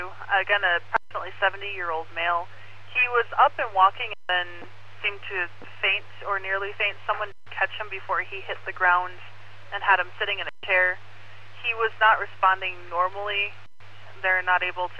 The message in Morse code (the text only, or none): none